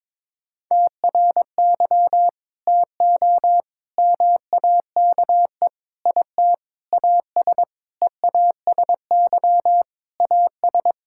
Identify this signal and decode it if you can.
Morse code, heard as a keyed tone, about 22 wpm, 700 Hz. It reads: TRY TO MAKE IT AS EASY AS
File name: practice / 22wpm / Sentences with 8-Word Phrases - 22wpm